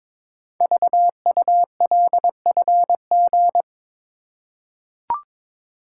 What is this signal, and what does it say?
Morse code, heard as a keyed tone, about 23 wpm, 700 Hz. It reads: VULFG
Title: Code Groups - 5 Letters - 22wpm